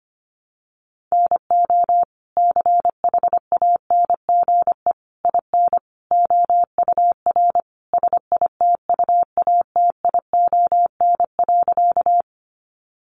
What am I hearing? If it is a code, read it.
Morse code, heard as a keyed tone, about 25 wpm, 700 Hz. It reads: NO CHANGE IN OUR SITUATION.
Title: Sentences from Top 1200 Words 25wpm